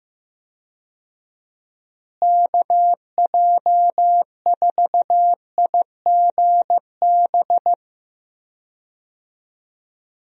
Morse code, heard as a keyed tone, about 14 wpm, 700 Hz. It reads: KJ4IGB
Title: USA Call Signs - 6 Characters - Repeated 3x - 15wpm